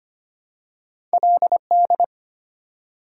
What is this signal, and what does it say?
Morse code, heard as a keyed tone, about 25 wpm, 700 Hz. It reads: LD
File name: Two Letter-Number Combinations - 25wpm